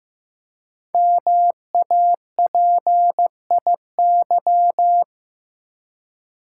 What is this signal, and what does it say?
Morse code, heard as a keyed tone, about 15 wpm, 700 Hz. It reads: MAPIY
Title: Code Groups - 5 Letters - 15wpm